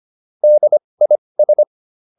Morse code, heard as a keyed tone, about 25 wpm, spaced as for 20 wpm, 600 Hz. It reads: DIS
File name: DIS7 20